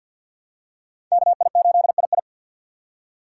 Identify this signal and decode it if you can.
Morse code, heard as a keyed tone, about 48 wpm, 700 Hz. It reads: KI8II
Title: USA Call Signs - 3x Character Spacing - 50wpm